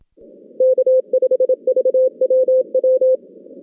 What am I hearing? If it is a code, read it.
Morse code, heard as a keyed tone, about 27 wpm, 520 Hz. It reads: K5VWW
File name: 2016 JAN CW NAQP / 2016 Jan naqp k5ot - 0-8